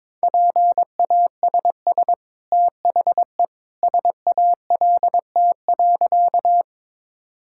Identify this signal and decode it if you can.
Morse code, heard as a keyed tone, about 21 wpm, 700 Hz. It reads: PASS THE SALT.
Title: Sentences from Top 1000 Words - Review Part 5 - 22wpm